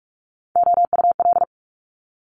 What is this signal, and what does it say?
Morse code, heard as a keyed tone, about 46 wpm, 700 Hz. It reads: OUL
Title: Multi-letter Phonemes Quick Fire 45wpm